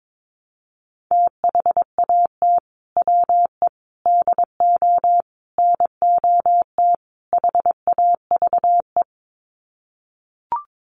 Morse code, heard as a keyed tone, about 22 wpm, 700 Hz. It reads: THAT WE DO NOT HAVE
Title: Sets of 5 Words Encore 22wpm